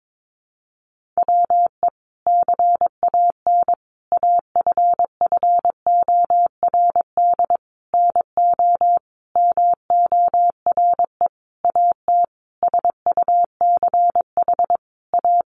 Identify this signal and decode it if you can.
Morse code, heard as a keyed tone, about 22 wpm, 700 Hz. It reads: WE CAN AFFORD NO MORE AT SUCH A
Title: Sentences from Top 1100 Words 22wpm